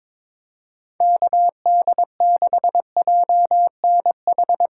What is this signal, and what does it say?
Morse code, heard as a keyed tone, about 22 wpm, 700 Hz. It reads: KD6JNH
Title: USA Call Signs - 6 Characters - 2x Character Spacing - 22wpm